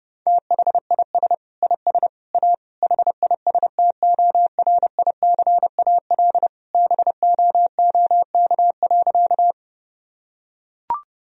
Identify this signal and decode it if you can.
Morse code, heard as a keyed tone, about 30 wpm, 700 Hz. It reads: THIS IS A HISTORICAL BOOK.
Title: Sentences from Top 1000 Words - Review Part 4 - 30wpm